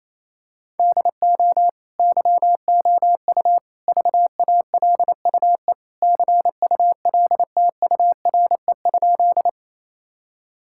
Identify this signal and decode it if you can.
Morse code, heard as a keyed tone, about 27 wpm, 700 Hz. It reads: DO YOU VALUE CULTURE?
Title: Sentences from Top 1000 Words - Review Part 6 - 28wpm